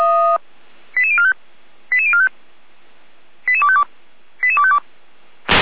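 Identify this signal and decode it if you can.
Five-tone selective call (ZVEI), heard as quick runs of five tones, tones of about 80 ms. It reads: 88045 88045 88252 88252
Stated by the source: ZVEI five-tone selective call, code 88352, received with an SDR.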